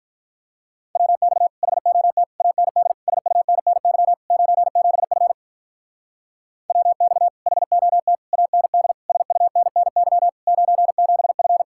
Morse code, heard as a keyed tone, about 46 wpm, 700 Hz. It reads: WX HOT AND SUNNY 97F WX HOT AND SUNNY 97F